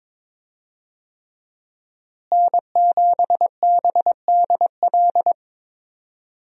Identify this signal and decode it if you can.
Morse code, heard as a keyed tone, about 22 wpm, 700 Hz. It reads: N7BDL
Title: USA Call Signs - 5 Characters - Repeated 3x - 22wpm